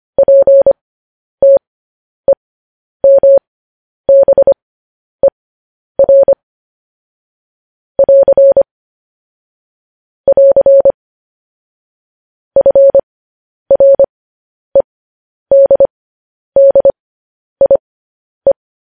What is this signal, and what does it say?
Morse code, heard as a keyed tone, about 25 wpm, spaced as for 10 wpm, 550 Hz. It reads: PTEMBER <AR> <AR> FREDDIE
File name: podcast / News headlines-1x-25x10